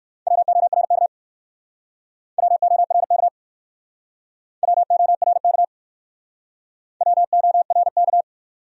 Morse code, heard as a keyed tone, about 59 wpm, 700 Hz. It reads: WORK WORK WORK WORK